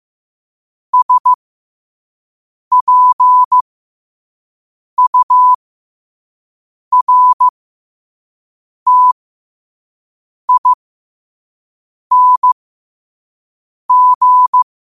Morse code, heard as a keyed tone, about 15 wpm, spaced as for 5 wpm, 1000 Hz. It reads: SPURTING
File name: qotdfiles / 5wpm